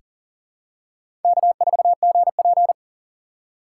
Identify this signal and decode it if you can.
Morse code, heard as a keyed tone, about 40 wpm, 700 Hz. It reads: K4GP
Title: USA Call Signs - 4 Characters - 4x Character Spacing - 40wpm